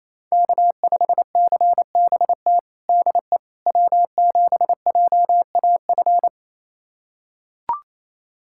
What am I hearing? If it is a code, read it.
Morse code, heard as a keyed tone, about 28 wpm, 700 Hz. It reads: K5CBT DE W7JAF
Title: USA Callsign DE Callsign - 5 Characters - 28wpm